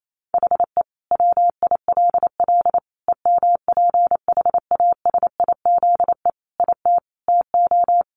Morse code, heard as a keyed tone, about 28 wpm, 700 Hz. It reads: HE WILL EMPHASIZE IT TO
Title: Sentences from Top 1500 Words 28wpm